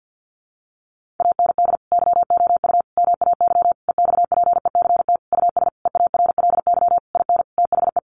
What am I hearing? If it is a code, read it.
Morse code, heard as a keyed tone, about 50 wpm, 700 Hz. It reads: AND YOU MAY EXPECT US EARLY IN THE
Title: Sentences from Top 1500 Words 50wpm